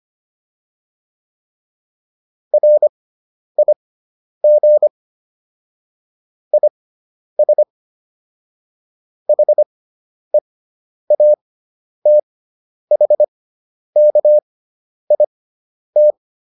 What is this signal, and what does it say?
Morse code, heard as a keyed tone, about 25 wpm, spaced as for 10 wpm, 600 Hz. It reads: RIG IS HEATHKIT
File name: QSO310 10